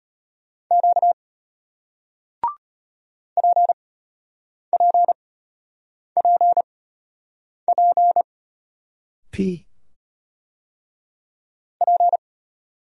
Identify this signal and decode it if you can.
Morse code, heard as a keyed tone, about 33 wpm, spaced as for 8 wpm, 700 Hz. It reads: Q PPPP P